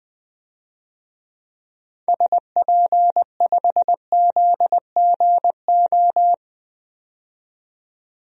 Morse code, heard as a keyed tone, about 20 wpm, 700 Hz. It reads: SP5ZGO
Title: DXCC-100 Call Signs - 6 Characters - 20wpm